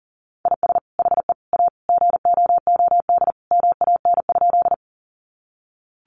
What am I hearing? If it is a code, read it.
Morse code, heard as a keyed tone, about 40 wpm, 700 Hz. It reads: IS HE A GOOD MAN?
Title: Sentences from Top 200 Words 40wpm